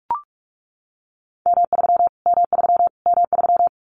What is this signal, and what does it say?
Morse code, heard as a keyed tone, about 44 wpm, 700 Hz. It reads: M3 M3 M3
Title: Callsign Prefixes for Top 25 DXCC - Repeated 3x - 45wpm